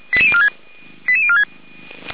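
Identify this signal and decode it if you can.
Five-tone selective call (ZVEI), heard as quick runs of five tones, tones of about 80 ms. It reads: 88046
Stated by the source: ZVEI five-tone selective call received by SDR.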